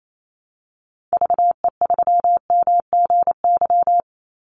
Morse code, heard as a keyed tone, about 28 wpm, 700 Hz. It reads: VE3MGY
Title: Call Signs Repeated 3x - 28wpm